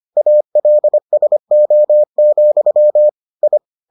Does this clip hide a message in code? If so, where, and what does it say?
Yes: Morse code from 0.2 to 3.6 s, heard as a keyed tone, about 25 wpm, 600 Hz. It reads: ALSO, I